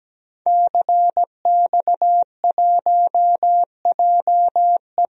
Morse code, heard as a keyed tone, about 17 wpm, 700 Hz. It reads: CX1JE
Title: DXCC-100 Call Signs - 5 Characters - Repeated 3x - 17wpm